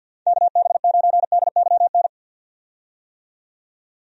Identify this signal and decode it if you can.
Morse code, heard as a keyed tone, about 51 wpm, 700 Hz. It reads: KB9DYN
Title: USA Call Signs - 6 Characters - 50wpm